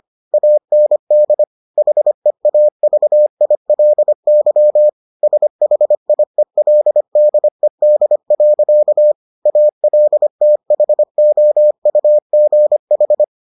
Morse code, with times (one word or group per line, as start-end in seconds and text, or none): 0.3-1.4 AND
1.8-4.9 HEAVILY
5.2-9.1 SHIELDED.
9.5-13.2 ALTHOUGH